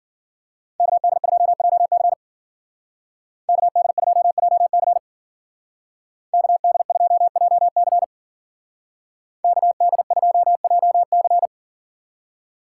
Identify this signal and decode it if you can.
Morse code, heard as a keyed tone, about 51 wpm, 700 Hz. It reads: KD2JC KD2JC KD2JC KD2JC